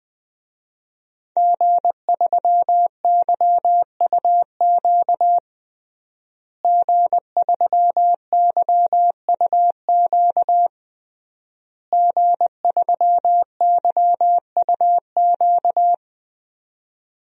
Morse code, heard as a keyed tone, about 20 wpm, 700 Hz. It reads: G3YUQ G3YUQ G3YUQ
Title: DXCC-100 Call Signs - 5 Characters - Repeated 3x - 20wpm